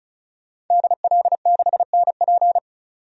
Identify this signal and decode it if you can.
Morse code, heard as a keyed tone, about 36 wpm, 700 Hz. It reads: DL6NP
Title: DXCC-100 Call Signs - 5 Characters - 35wpm